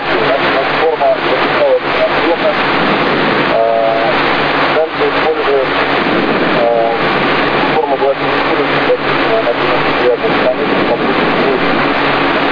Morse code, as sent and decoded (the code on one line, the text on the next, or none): none